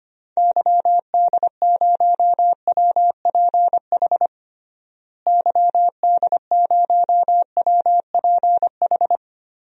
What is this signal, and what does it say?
Morse code, heard as a keyed tone, about 25 wpm, 700 Hz. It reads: YD0WPH YD0WPH
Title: DXCC-100 Call Signs - 6 Characters - Repeated 3x - 25wpm